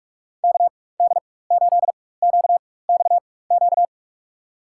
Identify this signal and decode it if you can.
Morse code, heard as a keyed tone, about 46 wpm, spaced as for 21 wpm, 700 Hz. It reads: KD8QXQ
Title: USA Call Signs - 6 Characters - 4x Character Spacing - 45wpm